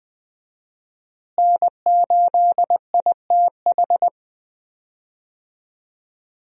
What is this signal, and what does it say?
Morse code, heard as a keyed tone, about 20 wpm, 700 Hz. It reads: N8ITH
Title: USA Call Signs - 5 Characters - Repeated 3x - 20wpm